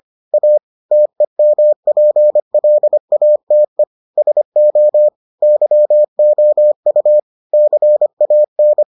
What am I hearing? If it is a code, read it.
Morse code, heard as a keyed tone, about 25 wpm, 600 Hz. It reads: A TEMPLATE SO YOU CAN